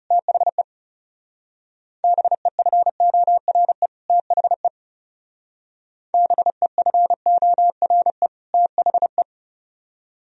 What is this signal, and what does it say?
Morse code, heard as a keyed tone, about 40 wpm, 700 Hz. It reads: THE BEFORE THE BEFORE THE